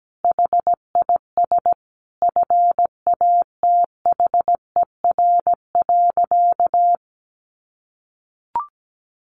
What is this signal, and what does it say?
Morse code, heard as a keyed tone, about 17 wpm, 700 Hz. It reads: HIS FATHER.
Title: Sentences from Top 1500 Words 17wpm